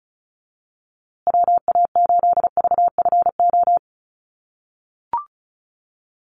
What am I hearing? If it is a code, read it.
Morse code, heard as a keyed tone, about 34 wpm, 700 Hz. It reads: WA8VFO
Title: USA Call Signs Repeated 3x 35wpm